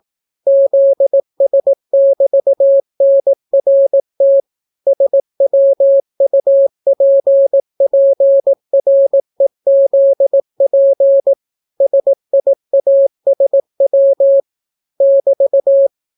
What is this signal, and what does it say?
Morse code, heard as a keyed tone, about 18 wpm, 550 Hz. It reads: ZS<BT>NRT SWUPPREZP SIASW <BT>